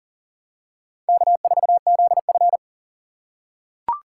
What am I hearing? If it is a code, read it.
Morse code, heard as a keyed tone, about 40 wpm, 700 Hz. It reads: K4ZF
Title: USA Call Signs - 4 Characters - 3x Character Spacing - 40wpm